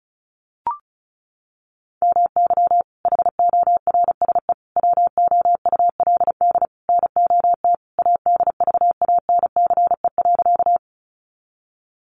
Morse code, heard as a keyed tone, about 34 wpm, 700 Hz. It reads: MY HORSE WOULD NOT ADVANCE.
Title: Sentences from Top 1400 Words 35wpm